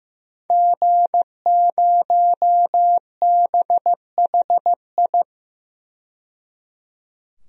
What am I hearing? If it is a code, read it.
Morse code, heard as a keyed tone, about 15 wpm, 700 Hz. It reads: G0BHI
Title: DXCC-100 Call Signs - 5 Characters - 15wpm